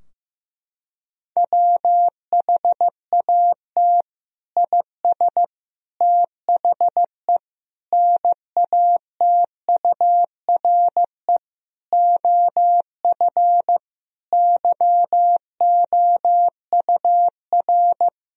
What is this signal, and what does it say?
Morse code, heard as a keyed tone, about 15 wpm, 700 Hz. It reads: WHAT IS THE NATURE OF YOUR
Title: Sentences from Top 1000 Words - Review Part 4 - 15wpm